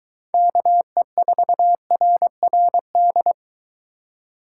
Morse code, heard as a keyed tone, about 21 wpm, 700 Hz. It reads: KE4RRD